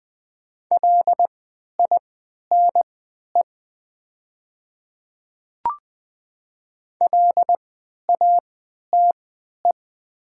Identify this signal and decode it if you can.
Morse code, heard as a keyed tone, about 20 wpm, spaced as for 11 wpm, 700 Hz. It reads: LINE LATE
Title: 3x Character Spacing - Top 500 Words - 20wpm